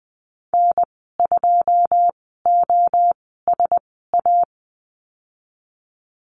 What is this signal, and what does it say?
Morse code, heard as a keyed tone, about 20 wpm, spaced as for 15 wpm, 700 Hz. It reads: N2OSA
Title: 088 Lesson 15g - Number 5 - Call Signs 20wpm